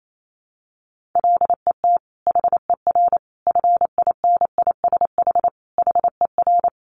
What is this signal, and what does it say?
Morse code, heard as a keyed tone, about 28 wpm, 700 Hz. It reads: LET HER FINISH HER
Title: Sentences from Top 1200 Words 28wpm